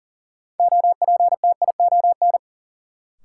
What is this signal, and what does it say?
Morse code, heard as a keyed tone, about 40 wpm, 700 Hz. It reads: OPTION
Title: Sentences from Top 1300 Words - New Word Drill Alt - 40wpm